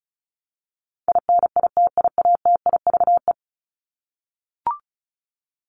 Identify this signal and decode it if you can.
Morse code, heard as a keyed tone, about 34 wpm, 700 Hz. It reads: INITIATIVE
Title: Top 1751-2000 Words 35wpm